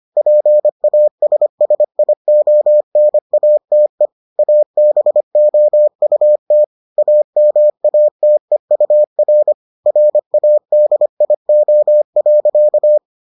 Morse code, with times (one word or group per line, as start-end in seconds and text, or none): 0.2-4.1 PASSIONATE
4.4-6.6 ABOUT
7.0-9.5 AMATEUR
9.9-13.0 RADIO.